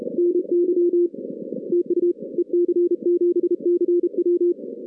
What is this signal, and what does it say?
Morse code, heard as a keyed tone, about 32 wpm, 340 Hz. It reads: NY TU EC7CW
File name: 2016 arrl dx cw / 2016-ARRL-DX-CW-K5TR- - 0-41